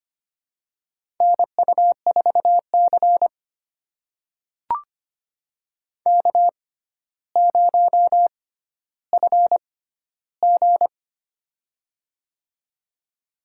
Morse code, heard as a keyed tone, about 25 wpm, spaced as for 13 wpm, 700 Hz. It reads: NU4C K0FG
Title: USA Call Signs - 4 Characters - 6x Character Spacing - 25wpm